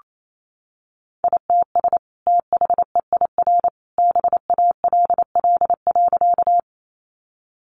Morse code, heard as a keyed tone, about 28 wpm, 700 Hz. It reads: ITS THEIR BALL.
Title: Sentences from Top 900 Words 28wpm